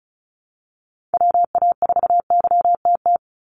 Morse code, heard as a keyed tone, about 36 wpm, 700 Hz. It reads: WA4YTT
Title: USA Call Signs 35wpm